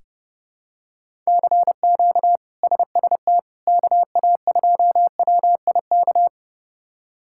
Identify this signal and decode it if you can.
Morse code, heard as a keyed tone, about 32 wpm, 700 Hz. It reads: CQ SST KA2WIK
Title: K1USN SST QSO Practice - 30wpm